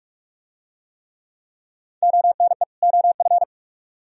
Morse code, heard as a keyed tone, about 44 wpm, 700 Hz. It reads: ONE OF